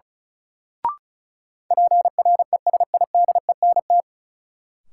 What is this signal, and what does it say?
Morse code, heard as a keyed tone, about 36 wpm, 700 Hz. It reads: PRESIDENT
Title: Sets of 2 Words - New Word Drill - 35wpm